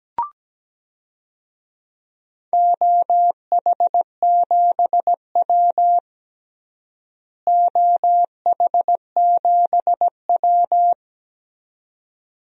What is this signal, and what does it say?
Morse code, heard as a keyed tone, about 17 wpm, 700 Hz. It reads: OH7W OH7W
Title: DXCC-100 Call Signs - 4 Characters - Repeated 2x - 17wpm